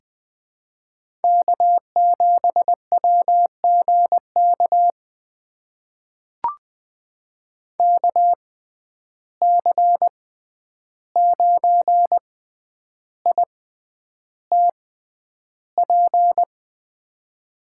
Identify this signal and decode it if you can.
Morse code, heard as a keyed tone, about 20 wpm, spaced as for 10 wpm, 700 Hz. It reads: K7WGK KC9ITP
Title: USA Call Signs - 6x Character Spacing - 20wpm